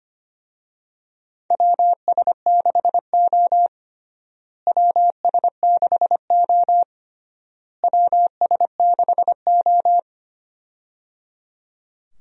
Morse code, heard as a keyed tone, about 25 wpm, 700 Hz. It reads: WS6O WS6O WS6O
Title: USA Call Signs - 4 Characters - Repeated 3x - 25wpm